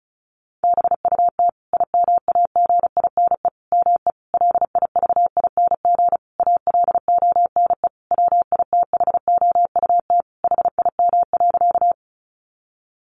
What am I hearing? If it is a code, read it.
Morse code, heard as a keyed tone, about 34 wpm, 700 Hz. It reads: BUT IMAGINE ME LIVING ALONE WITHOUT HIM.
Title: Sentences from Top 1400 Words 35wpm